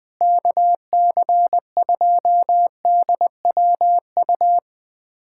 Morse code, heard as a keyed tone, about 20 wpm, 700 Hz. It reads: KC2DWU
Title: USA Callsign DE Callsign - 6 Characters - 20wpm